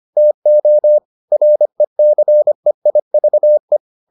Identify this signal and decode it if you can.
Morse code, heard as a keyed tone, about 25 wpm, 600 Hz. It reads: TO RECEIVE